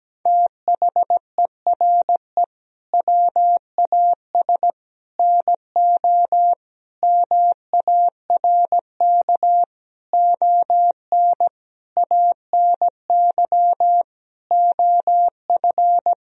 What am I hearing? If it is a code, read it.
Morse code, heard as a keyed tone, about 17 wpm, 700 Hz. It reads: THERE WAS NO MARK ON ANY OF
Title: Sentences from Top 1000 Words - Review Part 1 - 17wpm